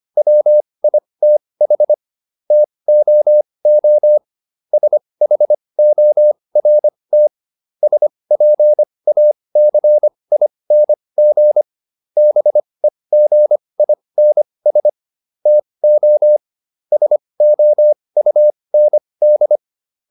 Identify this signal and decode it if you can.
Morse code, heard as a keyed tone, about 25 wpm, spaced as for 20 wpm, 600 Hz. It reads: WITH TOO SHORT SPACING BEGINS TO SOUND